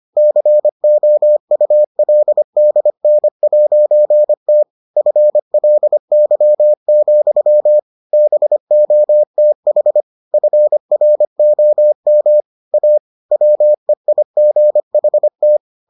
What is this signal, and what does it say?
Morse code, heard as a keyed tone, about 25 wpm, 600 Hz. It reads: COULDN'T FLY, BOTH FROM A WEIGHT